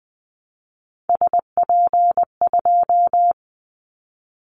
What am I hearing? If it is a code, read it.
Morse code, heard as a keyed tone, about 20 wpm, 700 Hz. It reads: SP2
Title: Callsign Prefixes for Top 25 DXCC - 20wpm